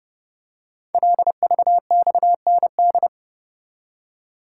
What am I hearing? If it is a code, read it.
Morse code, heard as a keyed tone, about 28 wpm, 700 Hz. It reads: LVXND
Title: Code Groups - 5 Letters - 30wpm